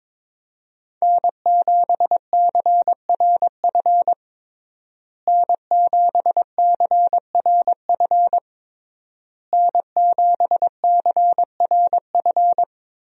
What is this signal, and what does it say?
Morse code, heard as a keyed tone, about 22 wpm, 700 Hz. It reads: N7CRF N7CRF N7CRF
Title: USA Call Signs - 5 Characters - Repeated 3x - 22wpm